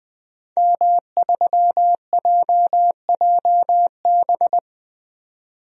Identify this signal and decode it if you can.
Morse code, heard as a keyed tone, about 20 wpm, 700 Hz. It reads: M3JJB
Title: DXCC-100 Call Signs - 5 Characters - 20wpm